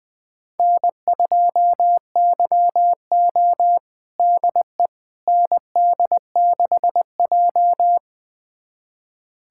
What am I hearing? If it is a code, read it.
Morse code, heard as a keyed tone, about 20 wpm, 700 Hz. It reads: N2YO DE ND6J
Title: USA Callsign DE Callsign - 4 Characters - 20wpm